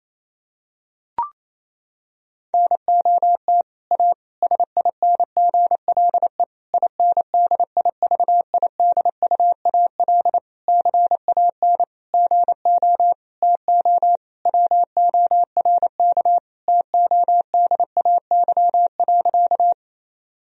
Morse code, heard as a keyed tone, about 28 wpm, 700 Hz. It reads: NOT A SINGLE INDIVIDUAL CAN GO TO WORK TODAY.
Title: Sentences from Top 1000 Words - Review Part 4 - 28wpm